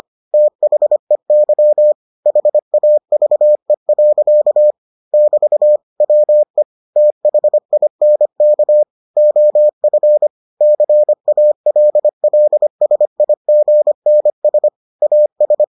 Morse code, heard as a keyed tone, about 25 wpm, 600 Hz. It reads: THEY HAVE. <BT> WE THINK OF CALLSIGNS AS